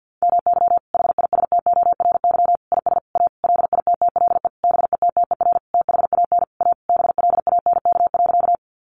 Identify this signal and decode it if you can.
Morse code, heard as a keyed tone, about 50 wpm, 700 Hz. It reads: MY HISTORY IS A LITTLE BETTER THAN A BLANK.
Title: Sentences from Top 1500 Words 50wpm